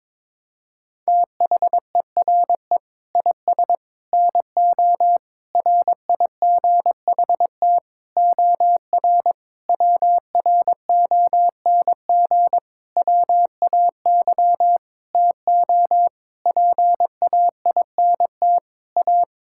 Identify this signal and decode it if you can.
Morse code, heard as a keyed tone, about 22 wpm, 700 Hz. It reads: THERE IS NO RIGHT OR WRONG WAY TO PAINT A
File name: practice / 22wpm / Sentences with 8-Word Phrases - 22wpm